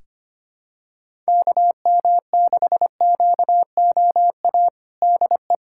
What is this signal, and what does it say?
Morse code, heard as a keyed tone, about 25 wpm, 700 Hz. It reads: KM6QOA DE
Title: USA Callsign DE Callsign - 6 Characters - 25wpm